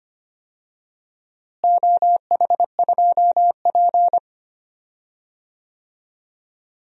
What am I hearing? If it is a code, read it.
Morse code, heard as a keyed tone, about 25 wpm, 700 Hz. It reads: OH2P
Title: DXCC-100 Call Signs - 4 Characters - Repeated 3x - 25wpm